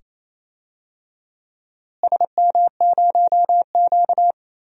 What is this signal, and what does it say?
Morse code, heard as a keyed tone, about 28 wpm, 700 Hz. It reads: SM0Q
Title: DXCC-100 Call Signs - 4 Characters - Repeated 2x - 28wpm